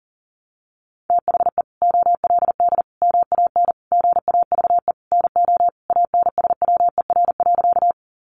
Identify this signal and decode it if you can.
Morse code, heard as a keyed tone, about 40 wpm, 700 Hz. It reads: THE OLD MAN GAVE NO ANSWER.
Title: Sentences from Top 500 Words 40wpm